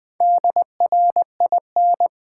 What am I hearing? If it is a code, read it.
Morse code, heard as a keyed tone, about 20 wpm, 700 Hz. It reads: DRIN
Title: Build A Word - 1 Letter At A Time - 5-Letter Words - 20wpm